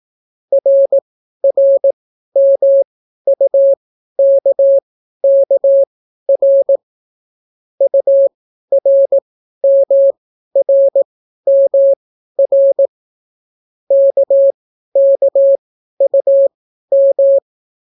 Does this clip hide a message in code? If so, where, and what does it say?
Yes: Morse code from 0.5 to 17.4 s, heard as a keyed tone, about 18 wpm, spaced as for 12 wpm, 550 Hz. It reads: RRMUKKR URMRMR KKUM